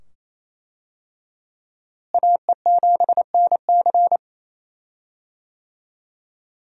Morse code, heard as a keyed tone, about 28 wpm, 700 Hz. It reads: AE7NC
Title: USA Call Signs - 5 Characters - Repeated 2x - 28wpm